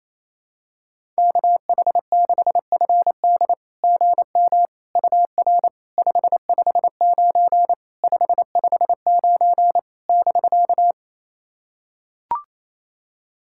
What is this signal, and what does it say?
Morse code, heard as a keyed tone, about 28 wpm, 700 Hz. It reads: KH6FD GM UR 559 559 <BK>
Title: SOTA QSO Practice 28wpm